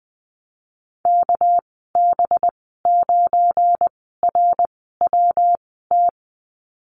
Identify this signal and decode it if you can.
Morse code, heard as a keyed tone, about 20 wpm, spaced as for 15 wpm, 700 Hz. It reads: KB9RWT
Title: 200 Lesson 32g - Letter k - Call Signs 20wpm